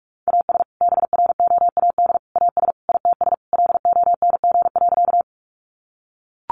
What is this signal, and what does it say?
Morse code, heard as a keyed tone, about 44 wpm, 700 Hz. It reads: AS BROAD AS ITS LONG.
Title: Sentences from Top 1000 Words 45wpm